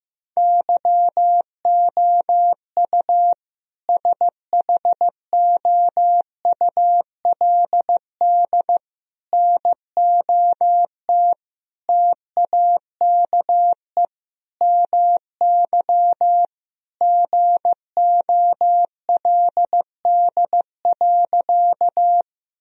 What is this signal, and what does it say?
Morse code, heard as a keyed tone, about 15 wpm, 700 Hz. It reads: YOU SHOULD NOT TAKE MY GOLD.
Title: Sentences from Top 1000 Words - Review Part 3 - 15wpm